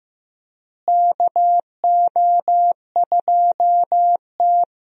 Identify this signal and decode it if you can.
Morse code, heard as a keyed tone, about 15 wpm, 700 Hz. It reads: KO2T
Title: USA Call Signs - 4 Characters - Repeated 3x - 15wpm